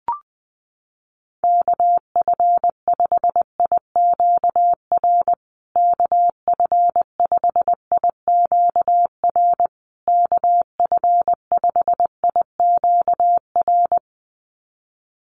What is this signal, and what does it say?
Morse code, heard as a keyed tone, about 20 wpm, 700 Hz. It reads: KF5IQR KF5IQR KF5IQR
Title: USA Call Signs Repeated 3x 20wpm